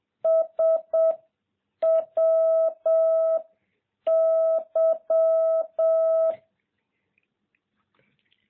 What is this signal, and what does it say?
Morse code, heard as a keyed tone, about 7 wpm, 660 Hz. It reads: S W Y